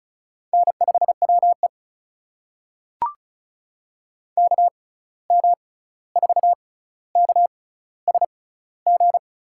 Morse code, heard as a keyed tone, about 36 wpm, spaced as for 16 wpm, 700 Hz. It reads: N5WE KM4KSG
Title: USA Call Signs - 6x Character Spacing - 35wpm